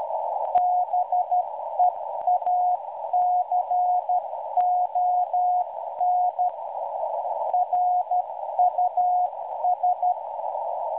Morse code, heard as a keyed tone, about 12 wpm, 720 Hz. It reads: BEACON RUS